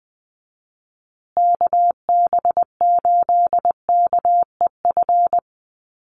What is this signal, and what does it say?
Morse code, heard as a keyed tone, about 20 wpm, 700 Hz. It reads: KB8KEF
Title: USA Call Signs Repeated 3x 20wpm